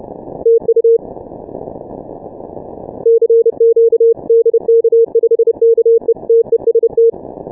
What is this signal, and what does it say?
Morse code, heard as a keyed tone, about 30 wpm, 460 Hz. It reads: TU CQ DK5KE TEST
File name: dk5ke / Contest2QSOs 150BpM